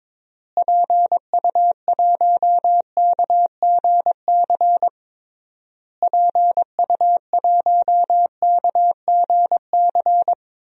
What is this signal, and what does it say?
Morse code, heard as a keyed tone, about 22 wpm, 700 Hz. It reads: PU1KGC PU1KGC
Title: DXCC-100 Call Signs - 6 Characters - Repeated 3x - 22wpm